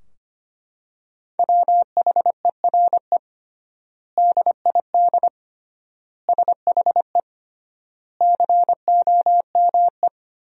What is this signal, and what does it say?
Morse code, heard as a keyed tone, about 25 wpm, 700 Hz. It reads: WHERE DID SHE COME
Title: 3x Word-Spacing - Sentences from Top 200 Words - 25wpm